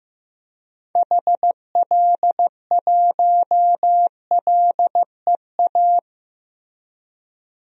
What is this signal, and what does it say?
Morse code, heard as a keyed tone, about 15 wpm, 700 Hz. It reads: HL1LEA
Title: DXCC-100 Call Signs - 6 Characters - 15wpm